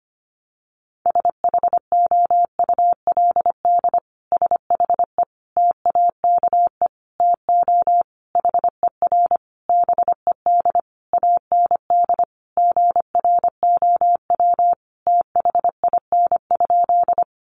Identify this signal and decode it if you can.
Morse code, heard as a keyed tone, about 25 wpm, 700 Hz. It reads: SHOULD SHE TAKE TO HER BED AND GROW THIN?
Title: Sentences from Top 1200 Words 25wpm